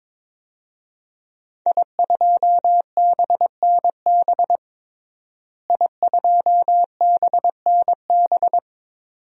Morse code, heard as a keyed tone, about 22 wpm, 700 Hz. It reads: I2BNB I2BNB
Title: DXCC-100 Call Signs - 5 Characters - Repeated 2x - 22wpm